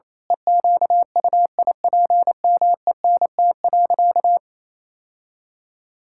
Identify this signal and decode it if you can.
Morse code, heard as a keyed tone, about 28 wpm, 700 Hz. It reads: EQUIPMENT.
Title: Sentences from Top 1000 Words - Review Part 6 - 28wpm